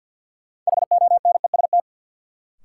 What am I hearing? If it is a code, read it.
Morse code, heard as a keyed tone, about 53 wpm, 700 Hz. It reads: HONEST
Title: Sets of 5 Words Encore - New Word Drill - 50wpm